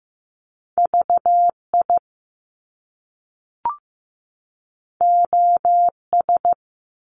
Morse code, heard as a keyed tone, about 15 wpm, 700 Hz. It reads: VI OS
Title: Top Two Letter Combinations in Words 15wpm